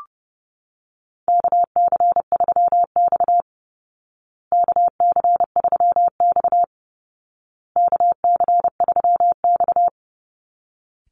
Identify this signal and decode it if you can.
Morse code, heard as a keyed tone, about 30 wpm, 700 Hz. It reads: KC3X KC3X KC3X
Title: Call Signs Repeated 2x - 30wpm